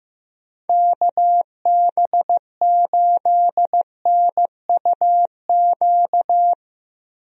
Morse code, heard as a keyed tone, about 15 wpm, 700 Hz. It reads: KB8NUQ
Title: USA Call Signs - 6 Characters - 15wpm